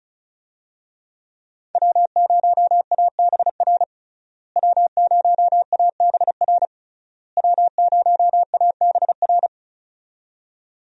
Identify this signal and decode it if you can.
Morse code, heard as a keyed tone, about 36 wpm, 700 Hz. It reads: W0ABR W0ABR W0ABR
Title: USA Call Signs - 5 Characters - Repeated 3x - 35wpm